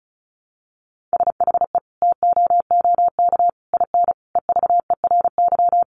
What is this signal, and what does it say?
Morse code, heard as a keyed tone, about 34 wpm, 700 Hz. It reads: SHE TOOK IN EVERY
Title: Sentences from Top 1500 Words 35wpm